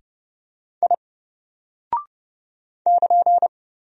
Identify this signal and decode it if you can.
Morse code, heard as a keyed tone, about 29 wpm, 700 Hz. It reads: I <KN>
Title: Single Character (Letter:Number:Punctuation:Prosign) 30wpm